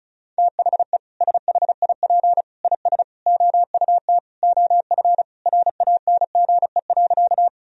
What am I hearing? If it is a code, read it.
Morse code, heard as a keyed tone, about 36 wpm, 700 Hz. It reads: THE SHIP IS OUT OF RANGE.
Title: Sentences from Top 1000 Words - Review Part 4 - 35wpm